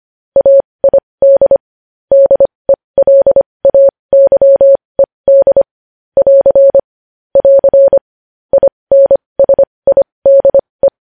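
Morse code, heard as a keyed tone, about 25 wpm, spaced as for 20 wpm, 550 Hz. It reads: AID DELAYED <AR> <AR> INSIDE
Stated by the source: Morse code news headlines practice at 20.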